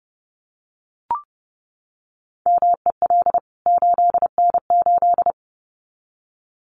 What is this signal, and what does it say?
Morse code, heard as a keyed tone, about 30 wpm, 700 Hz. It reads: MEL 8N8
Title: CWT Exchange 30wpm